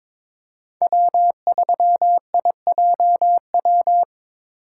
Morse code, heard as a keyed tone, about 22 wpm, 700 Hz. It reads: W3IJW
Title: USA Call Signs - 5 Characters - 3x Character Spacing - 22wpm